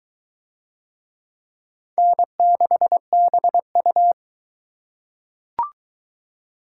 Morse code, heard as a keyed tone, about 21 wpm, 700 Hz. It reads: N6BU